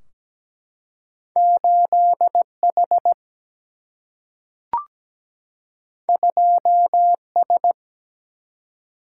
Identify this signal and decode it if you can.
Morse code, heard as a keyed tone, about 17 wpm, 700 Hz. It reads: 8 H 2 S
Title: Single-Letter Single-Number 17wpm